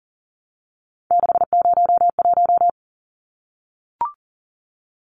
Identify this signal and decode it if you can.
Morse code, heard as a keyed tone, about 40 wpm, 700 Hz. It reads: 601
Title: Numbers - 3-Digit 40wpm